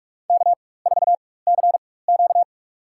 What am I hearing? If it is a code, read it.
Morse code, heard as a keyed tone, about 43 wpm, spaced as for 21 wpm, 700 Hz. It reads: K4CQ
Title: USA Call Signs - 4 Characters - 4x Character Spacing - 45wpm